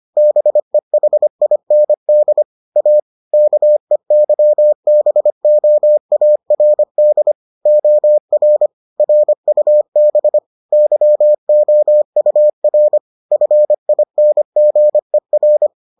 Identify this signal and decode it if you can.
Morse code, heard as a keyed tone, about 25 wpm, 600 Hz. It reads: BEHIND A KEYBOARD OR RUB YOUR FINGER